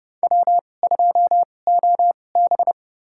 Morse code, heard as a keyed tone, about 30 wpm, spaced as for 22 wpm, 700 Hz. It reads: W2OB
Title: USA Call Signs - 4 Characters - 2x Character Spacing - 30wpm